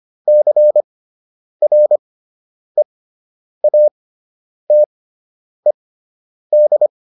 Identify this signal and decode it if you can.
Morse code, heard as a keyed tone, about 25 wpm, spaced as for 9 wpm, 600 Hz. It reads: CREATED